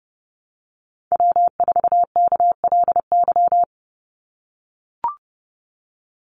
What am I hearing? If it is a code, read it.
Morse code, heard as a keyed tone, about 30 wpm, 700 Hz. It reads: W4KLY
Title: Call Signs - 30wpm